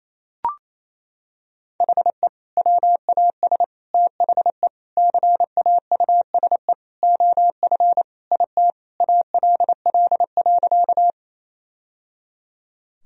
Morse code, heard as a keyed tone, about 28 wpm, 700 Hz. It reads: HE WAS THE CAUSE OF IT ALL.
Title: Sentences from Top 1000 Words - Review Part 1 - 28wpm